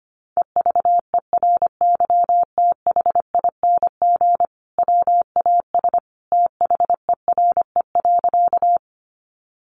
Morse code, heard as a keyed tone, about 25 wpm, 700 Hz. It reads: EVERYTHING WAS THERE.
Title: Sentences from Top 700 Words 25wpm